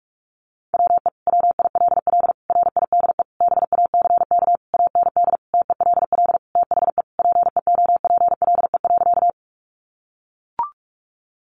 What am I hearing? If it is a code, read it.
Morse code, heard as a keyed tone, about 44 wpm, 700 Hz. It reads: WE WILL RIDE BACK AND TELL THE PEOPLE.
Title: Sentences from Top 1200 Words 45wpm